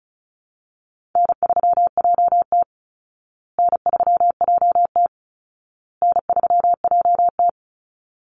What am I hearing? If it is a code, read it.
Morse code, heard as a keyed tone, about 34 wpm, 700 Hz. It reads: N3JT N3JT N3JT
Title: Easy Call Signs Repeated 3x - 35wpm